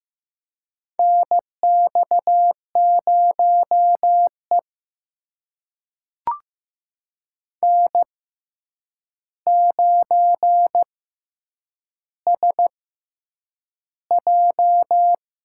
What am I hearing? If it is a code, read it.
Morse code, heard as a keyed tone, about 15 wpm, spaced as for 8 wpm, 700 Hz. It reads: NX0E N9SJ
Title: USA Call Signs - 6x Character Spacing - 15wpm